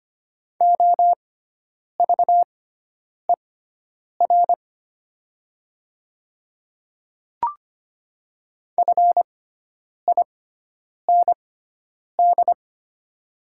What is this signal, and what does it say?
Morse code, heard as a keyed tone, about 25 wpm, spaced as for 9 wpm, 700 Hz. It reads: OVER FIND
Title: 6x Character Spacing - Top 100 Words - 25wpm